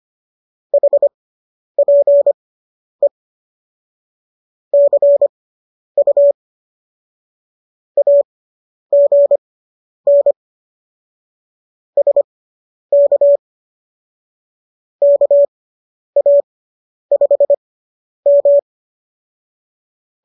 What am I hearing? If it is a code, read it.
Morse code, heard as a keyed tone, about 25 wpm, spaced as for 10 wpm, 570 Hz. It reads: HPE CU AGN SK KA5M